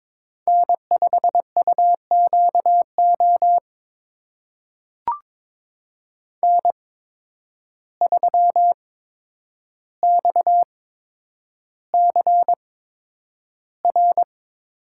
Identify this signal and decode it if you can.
Morse code, heard as a keyed tone, about 22 wpm, spaced as for 9 wpm, 700 Hz. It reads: N5UQO N3XCR
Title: USA Call Signs - 5 Characters - 8x Character Spacing - 22wpm